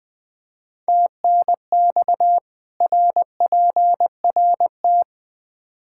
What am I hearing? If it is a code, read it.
Morse code, heard as a keyed tone, about 20 wpm, 700 Hz. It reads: TNX RPRT
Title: Practice QSO - Exchange 2 - Info - 20wpm